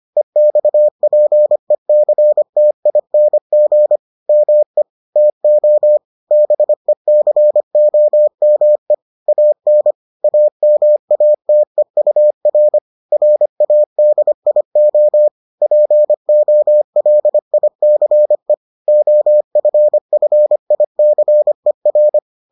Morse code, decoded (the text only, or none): EXPECTING ME TO BECOME AN AMATEUR RADIO POLICE OFFICER